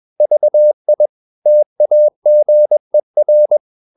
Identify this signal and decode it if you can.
Morse code, heard as a keyed tone, about 21 wpm, 600 Hz. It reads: VI TAGER
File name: Ferie QRS 21wpm